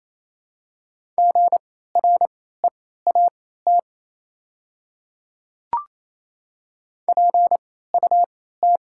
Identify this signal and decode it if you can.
Morse code, heard as a keyed tone, about 28 wpm, spaced as for 16 wpm, 700 Hz. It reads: GREAT PUT
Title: 3x Character Spacing - Top 200 Words - 28wpm